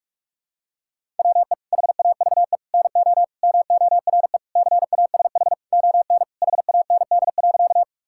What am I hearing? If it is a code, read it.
Morse code, heard as a keyed tone, about 46 wpm, 700 Hz. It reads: WE HAVE NO MORE CASH ON HAND.
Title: Sentences from Top 1000 Words - Review Part 5 - 45wpm